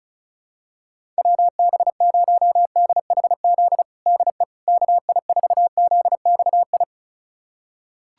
Morse code, heard as a keyed tone, about 34 wpm, 700 Hz. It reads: WB0DHZ DE KI4ZXI
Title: USA Callsign DE Callsign - 6 Characters - 35wpm